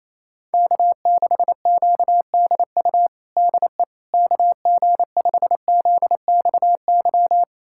Morse code, heard as a keyed tone, about 28 wpm, 700 Hz. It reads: K6QDU DE KG5ZXY
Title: USA Callsign DE Callsign 28wpm